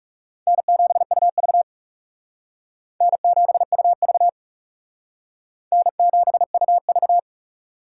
Morse code, heard as a keyed tone, about 40 wpm, 700 Hz. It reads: N7UV N7UV N7UV